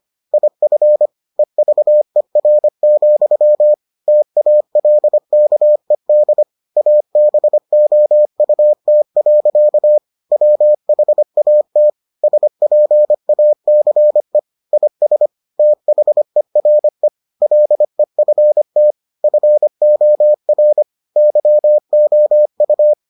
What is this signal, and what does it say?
Morse code, heard as a keyed tone, about 25 wpm, 600 Hz. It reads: IF EVER, TALKED ABOUT. WHAT SPACE IS THERE LEFT FOR YOU